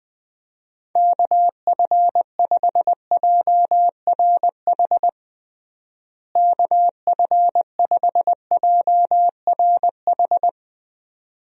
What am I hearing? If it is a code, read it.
Morse code, heard as a keyed tone, about 20 wpm, 700 Hz. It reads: KF5JRH KF5JRH
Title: USA Call Signs - 6 Characters - Repeated 2x - 20wpm